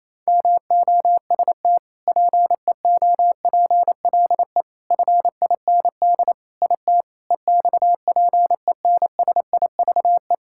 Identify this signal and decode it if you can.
Morse code, heard as a keyed tone, about 28 wpm, 700 Hz. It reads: MOST PEOPLE FIND IT EXPENSIVE
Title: Sentences from Top 1000 Words - Review Part 6 - 28wpm